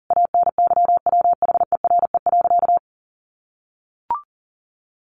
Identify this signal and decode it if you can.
Morse code, heard as a keyed tone, about 37 wpm, 700 Hz. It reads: ANYWHERE.
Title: Sentences from Top 1400 Words 40wpm